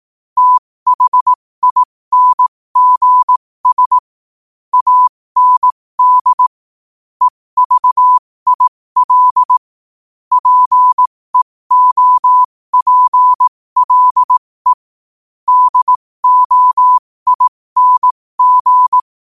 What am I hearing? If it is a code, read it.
Morse code, heard as a keyed tone, about 18 wpm, spaced as for 15 wpm, 1000 Hz. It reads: THINGS AND EVIL PEOPLE DOING
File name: qotdfiles / 15wpm